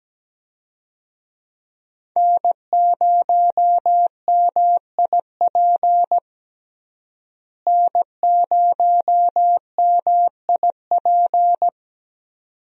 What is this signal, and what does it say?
Morse code, heard as a keyed tone, about 17 wpm, 700 Hz. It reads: N0MIP N0MIP
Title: USA Call Signs - 5 Characters - Repeated 2x - 17wpm